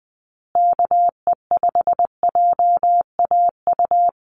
Morse code, heard as a keyed tone, about 20 wpm, 700 Hz. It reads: KE5JAU
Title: USA Call Signs Repeated 2x 20wpm